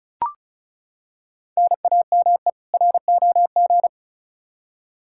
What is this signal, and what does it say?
Morse code, heard as a keyed tone, about 36 wpm, 700 Hz. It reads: NAME ROG
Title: Practice QSO - Exchange 1 - Beginning - 35wpm